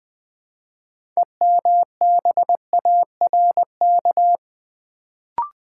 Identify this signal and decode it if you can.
Morse code, heard as a keyed tone, about 20 wpm, 700 Hz. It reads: EMBARK
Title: Build A Word - 2 Letters At A Time - 6-Letter Words - 20wpm